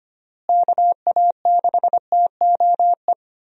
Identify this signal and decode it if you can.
Morse code, heard as a keyed tone, about 25 wpm, 700 Hz. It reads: KA6TOE
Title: USA Call Signs - 6 Characters - Repeated 2x - 25wpm